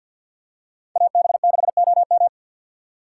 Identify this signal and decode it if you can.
Morse code, heard as a keyed tone, about 50 wpm, 700 Hz. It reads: AB6OM
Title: USA Call Signs - 5 Characters - 2x Character Spacing - 50wpm